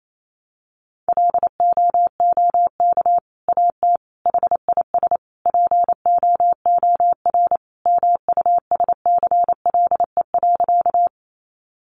Morse code, heard as a keyed tone, about 28 wpm, 700 Hz. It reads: LOOK AT HIS POOR MUSCLE.
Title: Sentences from Top 1300 Words 28wpm